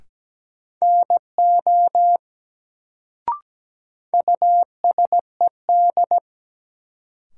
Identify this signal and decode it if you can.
Morse code, heard as a keyed tone, about 17 wpm, 700 Hz. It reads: NO USED
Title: Sets of 2 Words - New Word Drill - 17wpm